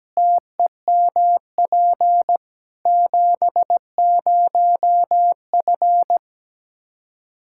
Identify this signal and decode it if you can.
Morse code, heard as a keyed tone, about 17 wpm, 700 Hz. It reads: TEMP 70F
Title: K1USN SST QSO Practice 17wpm